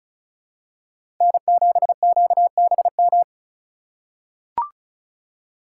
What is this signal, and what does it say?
Morse code, heard as a keyed tone, about 36 wpm, 690 Hz. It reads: N7QBM
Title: USA Call Signs - 5 Characters - 3x Character Spacing - 35wpm